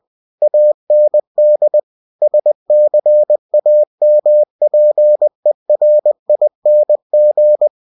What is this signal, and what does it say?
Morse code, heard as a keyed tone, about 20 wpm, 600 Hz. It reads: AND SCAMPERING